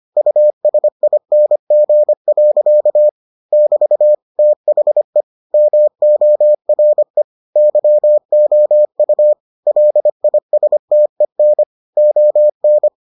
Morse code, heard as a keyed tone, about 25 wpm, 600 Hz. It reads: USING. <BT> THE MORE YOU LISTEN ON